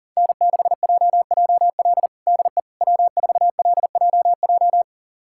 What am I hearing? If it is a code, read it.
Morse code, heard as a keyed tone, about 37 wpm, 700 Hz. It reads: N6JJL DE W4LJJ
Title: USA Callsign DE Callsign - 5 Characters - 40wpm